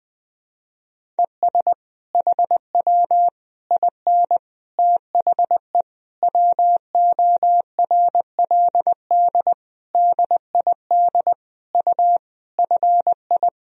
Morse code, heard as a keyed tone, about 21 wpm, 700 Hz. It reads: ES HW IN THE WORLD DID U FI
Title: Sentences from Top 200 Words and Abbreviations 20wpm